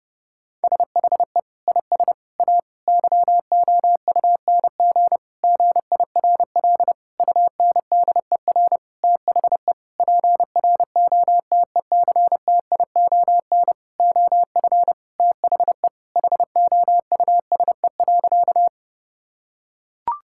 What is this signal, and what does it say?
Morse code, heard as a keyed tone, about 30 wpm, 700 Hz. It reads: SHE IS A YOUNG GIRL UNDER THE PROTECTION OF THE HOUSE.
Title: Sentences from Top 1600 Words 30wpm